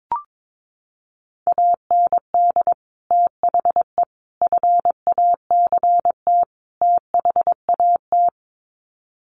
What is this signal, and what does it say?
Morse code, heard as a keyed tone, about 22 wpm, 700 Hz. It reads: AND THE FACT THAT
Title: Sets of 4 Words 22wpm